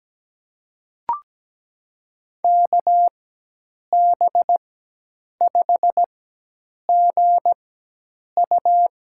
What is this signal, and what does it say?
Morse code, heard as a keyed tone, about 17 wpm, spaced as for 8 wpm, 700 Hz. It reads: KB5GU
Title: USA Call Signs - 5 Characters - 4x Character Spacing - 17wpm